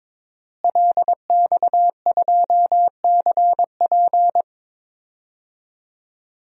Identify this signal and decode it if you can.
Morse code, heard as a keyed tone, about 22 wpm, 700 Hz. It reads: LX2CP
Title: DXCC-100 Call Signs - 5 Characters - Repeated 2x - 22wpm